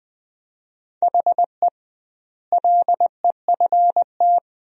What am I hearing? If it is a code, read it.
Morse code, heard as a keyed tone, about 21 wpm, 700 Hz. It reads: HE LEFT
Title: 2x Word-Spacing - Sentences from Top 500 Words - 20wpm